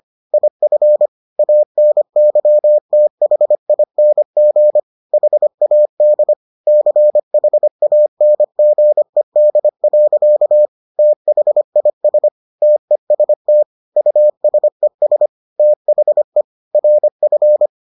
Morse code, heard as a keyed tone, about 25 wpm, 600 Hz. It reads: IF ANYTHING HAD CHANGED. THIS TEST USES THE RF